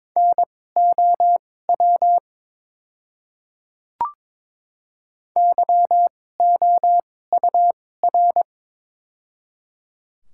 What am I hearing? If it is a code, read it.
Morse code, heard as a keyed tone, about 22 wpm, spaced as for 16 wpm, 700 Hz. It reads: NOW YOUR
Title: 2x Character Spacing - Top 100 Words - 22wpm